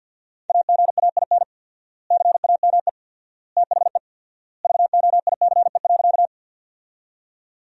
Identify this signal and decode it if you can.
Morse code, heard as a keyed tone, about 50 wpm, 700 Hz. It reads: AGAIN CAME THE VOICE.
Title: 4x Word-Spacing - Sentences from Top 500 Words - 50wpm